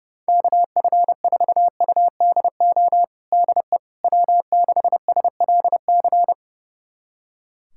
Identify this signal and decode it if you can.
Morse code, heard as a keyed tone, about 30 wpm, 700 Hz. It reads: KF4UDO DE W6SLC
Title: USA Callsign DE Callsign 30wpm